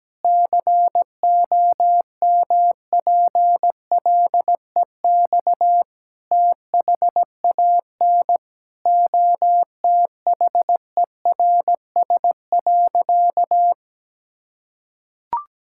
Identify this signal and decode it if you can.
Morse code, heard as a keyed tone, about 17 wpm, 700 Hz. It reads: COMPLEX THAN OTHERS.
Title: Sentences from Top 1000 Words - Review Part 6 - 17wpm